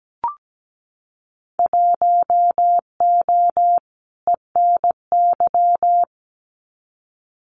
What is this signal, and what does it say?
Morse code, heard as a keyed tone, about 17 wpm, 700 Hz. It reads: 1O ENY
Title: Winter Field Day Exchanges 17wpm